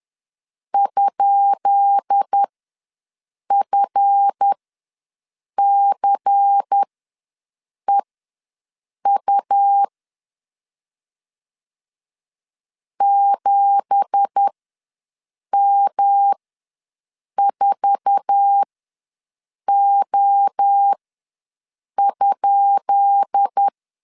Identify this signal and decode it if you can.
Morse code, heard as a keyed tone, about 10 wpm, spaced as for 6 wpm, 800 Hz. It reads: ?FCEU 7M4O?
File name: LEZIONI / 10 SIGNS / 10B INT ws 10wpm 4134